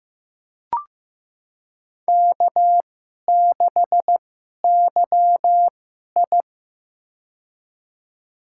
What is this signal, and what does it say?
Morse code, heard as a keyed tone, about 15 wpm, spaced as for 11 wpm, 700 Hz. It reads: K6YI
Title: USA Call Signs - 2x Character Spacing - 15wpm